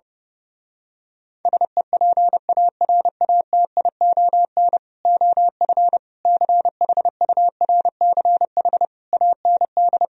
Morse code, heard as a keyed tone, about 28 wpm, 700 Hz. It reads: SEPARATION OF CHURCH AND
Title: Speed-Racing - Sets of 5 Words 30wpm